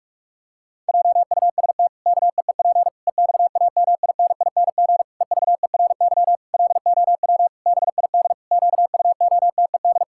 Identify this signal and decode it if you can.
Morse code, heard as a keyed tone, about 46 wpm, 700 Hz. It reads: JUST KEEP EXAMINING EVERY LOW BID QUOTED
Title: Pangram Challenge v2 - 45wpm